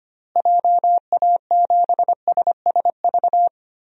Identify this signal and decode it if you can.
Morse code, heard as a keyed tone, about 25 wpm, 700 Hz. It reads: JA7SSV
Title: DXCC-100 Call Signs - 6 Characters - Repeated 3x - 25wpm